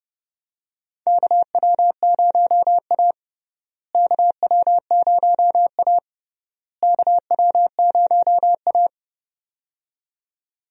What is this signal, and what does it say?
Morse code, heard as a keyed tone, about 32 wpm, 700 Hz. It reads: KW0A KW0A KW0A
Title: USA Call Signs - 4 Characters - Repeated 3x - 30wpm